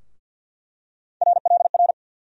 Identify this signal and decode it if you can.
Morse code, heard as a keyed tone, about 50 wpm, 700 Hz. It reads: RLR
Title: USA Call Sign Postfixes - 3 Letters 50wpm